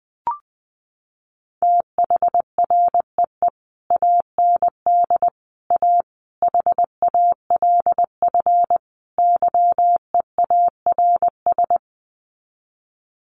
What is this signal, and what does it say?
Morse code, heard as a keyed tone, about 20 wpm, 700 Hz. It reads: THREE AND A HALF YEARS
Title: Sets of 5 Words Encore2 20wpm